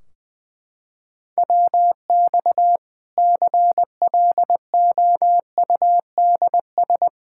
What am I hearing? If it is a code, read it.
Morse code, heard as a keyed tone, about 21 wpm, 700 Hz. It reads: WX CLOUDS
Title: K1USN SST QSO Practice 20wpm